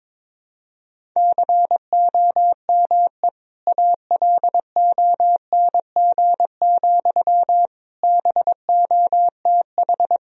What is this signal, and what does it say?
Morse code, heard as a keyed tone, about 22 wpm, 700 Hz. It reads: COME ALONG, BOTH
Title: Sentences from Top 1000 Words - Review Part 1 - 22wpm